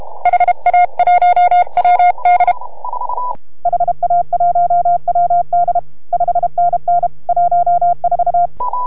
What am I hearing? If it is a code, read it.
Morse code, heard as a keyed tone, about 31 wpm, 690 Hz. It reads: HA1WD HA1WD 5NN 14